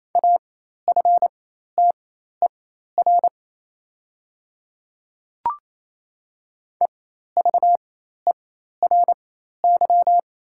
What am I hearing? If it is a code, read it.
Morse code, heard as a keyed tone, about 28 wpm, spaced as for 13 wpm, 700 Hz. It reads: AFTER EVERY
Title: 4x Character Spacing - Top 200 Words - 28wpm